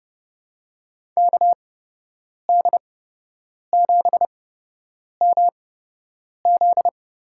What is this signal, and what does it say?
Morse code, heard as a keyed tone, about 32 wpm, spaced as for 8 wpm, 700 Hz. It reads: KD7MZ
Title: USA Call Signs - 8x Character Spacing - 30wpm